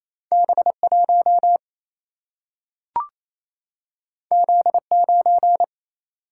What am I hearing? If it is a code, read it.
Morse code, heard as a keyed tone, about 28 wpm, 700 Hz. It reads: B1 Z9
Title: Single-Letter Single-Number 28wpm